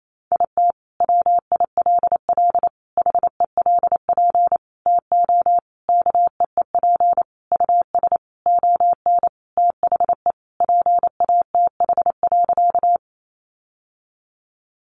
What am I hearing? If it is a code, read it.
Morse code, heard as a keyed tone, about 28 wpm, 700 Hz. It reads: IT WILL HELP TO KEEP US ON THE PATH.
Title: Sentences from Top 1400 Words 28wpm